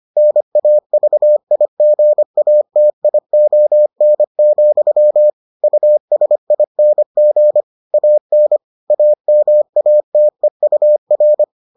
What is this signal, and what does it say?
Morse code, heard as a keyed tone, about 25 wpm, 600 Hz. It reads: NAVIGATION, USING AN AMATEUR